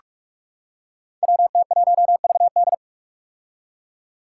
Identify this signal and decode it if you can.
Morse code, heard as a keyed tone, about 46 wpm, 700 Hz. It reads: WT1VD
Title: Five Challenging Letter-Number Combinations 45wpm